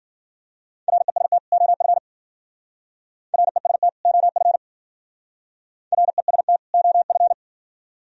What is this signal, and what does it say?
Morse code, heard as a keyed tone, about 49 wpm, 690 Hz. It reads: REST OF REST OF REST OF
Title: Speed-Racing - Sets of 2 Words 40wpm